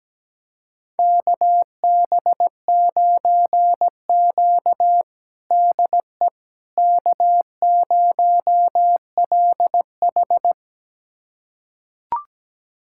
Morse code, heard as a keyed tone, about 17 wpm, 700 Hz. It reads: KB9Q DE K0LH
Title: USA Callsign DE Callsign - 4 Characters - 17wpm